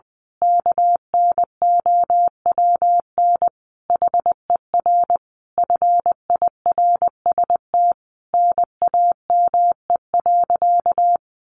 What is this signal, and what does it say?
Morse code, heard as a keyed tone, about 20 wpm, 700 Hz. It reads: KNOWN HER FIRST NAME.
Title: Sentences from Top 1200 Words 20wpm